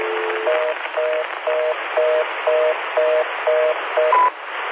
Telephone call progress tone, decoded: reorder tone